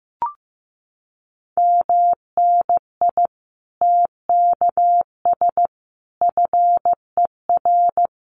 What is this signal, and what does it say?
Morse code, heard as a keyed tone, about 15 wpm, 700 Hz. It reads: MNI TKS FER
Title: Common QSO Phrases 15wpm